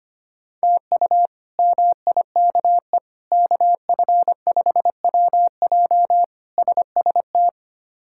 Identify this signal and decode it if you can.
Morse code, heard as a keyed tone, about 25 wpm, 700 Hz. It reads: TU MIKE KF5WJ SST
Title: K1USN SST QSO Practice - 25wpm